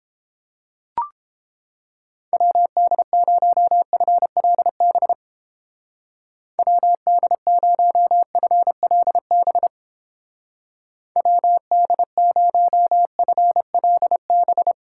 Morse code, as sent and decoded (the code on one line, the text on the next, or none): .-- -.. ----- ..-. .-.. -... / .-- -.. ----- ..-. .-.. -... / .-- -.. ----- ..-. .-.. -...
WD0FLB WD0FLB WD0FLB